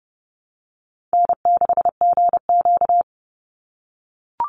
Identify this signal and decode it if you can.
Morse code, heard as a keyed tone, about 30 wpm, 700 Hz. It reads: N6GQ
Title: Easy Call Signs - 30wpm